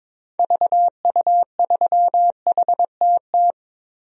Morse code, heard as a keyed tone, about 21 wpm, 700 Hz. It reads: VU3HTT
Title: DXCC-100 Call Signs - 6 Characters - Repeated 2x - 22wpm